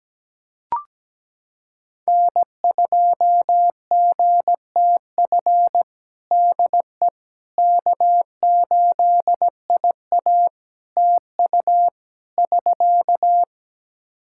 Morse code, heard as a keyed tone, about 17 wpm, 700 Hz. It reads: N2GTF DE K8IA TU <SK>
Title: Practice QSO - Exchange 4 - Ending - 17wpm